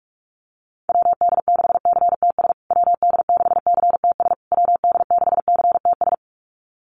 Morse code, heard as a keyed tone, about 44 wpm, 700 Hz. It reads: WD6CTS WD6CTS WD6CTS
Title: USA Call Signs Repeated 3x 45wpm